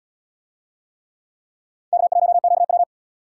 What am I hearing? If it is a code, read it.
Morse code, heard as a keyed tone, about 77 wpm, 700 Hz. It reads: K1QW